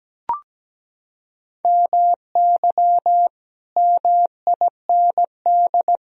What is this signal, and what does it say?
Morse code, heard as a keyed tone, about 17 wpm, 700 Hz. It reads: MY MIND
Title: Sentences from Top 1000 Words - Review Part 1 - 17wpm